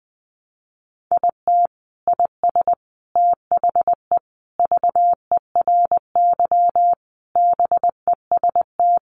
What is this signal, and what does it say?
Morse code, heard as a keyed tone, about 20 wpm, 700 Hz. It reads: IT IS THE VERY BEST
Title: Sentences from Top 1300 Words 20wpm